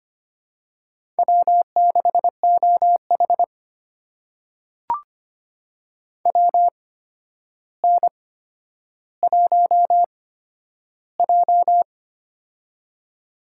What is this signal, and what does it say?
Morse code, heard as a keyed tone, about 25 wpm, spaced as for 11 wpm, 700 Hz. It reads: W6OH WN1J
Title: USA Call Signs - 4 Characters - 8x Character Spacing - 25wpm